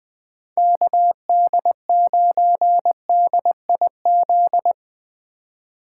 Morse code, heard as a keyed tone, about 23 wpm, 700 Hz. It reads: KD9DIZ
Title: Speed Racing - USA 2x3 Call Signs 20wpm